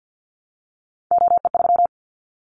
Morse code, heard as a keyed tone, about 50 wpm, 700 Hz. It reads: OE3
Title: Callsign Prefixes for Top 25 DXCC - Repeated 3x - 50wpm